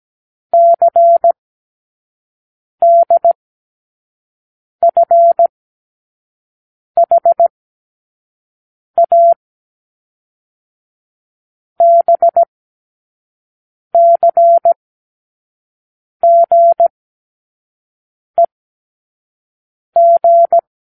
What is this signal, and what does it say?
Morse code, heard as a keyed tone, about 17 wpm, spaced as for 5 wpm, 680 Hz. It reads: CDFHA BCGEG
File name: k7qo / Lesson12(A-H)